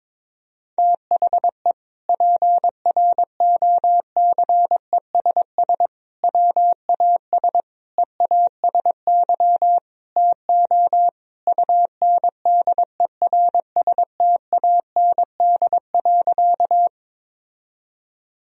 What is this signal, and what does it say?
Morse code, heard as a keyed tone, about 22 wpm, 700 Hz. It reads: THE PROCESS WAS EASY TO UNDERSTAND.
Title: Sentences from Top 1000 Words - Review Part 3 - 22wpm